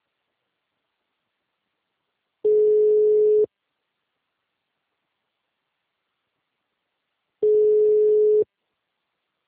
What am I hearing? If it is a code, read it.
Ringback tone (Europe, 425 Hz).